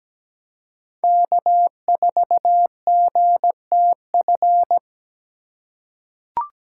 Morse code, heard as a keyed tone, about 17 wpm, 700 Hz. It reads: K4GTF
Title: USA Call Signs - 5 Characters - 4x Character Spacing - 17wpm